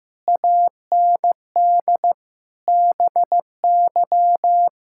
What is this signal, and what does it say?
Morse code, heard as a keyed tone, about 15 wpm, 700 Hz. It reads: AND BY
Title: Pangram Challenge v2 - 15wpm